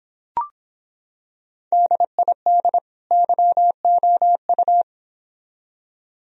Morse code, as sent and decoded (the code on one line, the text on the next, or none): -.. .. -.. / -.-- --- ..-
DID YOU